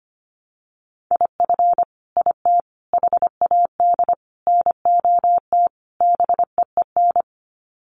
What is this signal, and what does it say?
Morse code, heard as a keyed tone, about 25 wpm, 690 Hz. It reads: IF IT HAD NOT BEEN
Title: Sets of 5 Words Encore 25wpm